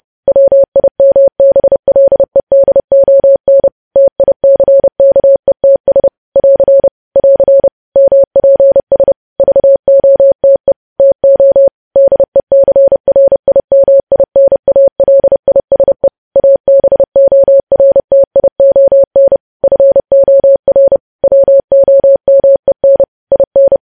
Morse code, heard as a keyed tone, about 30 wpm, 550 Hz. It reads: WIMBLEDON TICKETS <AR> <AR> MPS VOTE TO DECRIMINALISE ABORTION FOR WOMEN IN
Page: news headlines in morse code at 30 wpm